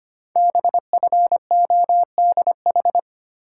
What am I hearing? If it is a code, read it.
Morse code, heard as a keyed tone, about 25 wpm, 690 Hz. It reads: BFODH
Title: Code Groups - 5 Letters - 25wpm